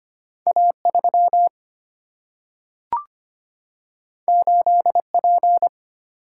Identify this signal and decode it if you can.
Morse code, heard as a keyed tone, about 25 wpm, 700 Hz. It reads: A3 8P
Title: Single-Letter Single-Number 25wpm